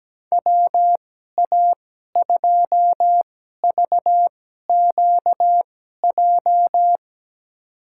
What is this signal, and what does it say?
Morse code, heard as a keyed tone, about 17 wpm, spaced as for 12 wpm, 700 Hz. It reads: WA2VQJ
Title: USA Call Signs - 6 Characters - 2x Character Spacing - 17wpm